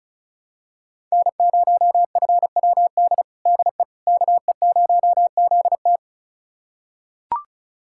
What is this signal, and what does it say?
Morse code, heard as a keyed tone, about 34 wpm, 700 Hz. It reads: N0FWD DE KE0ZT
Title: USA Callsign DE Callsign - 5 Characters - 35wpm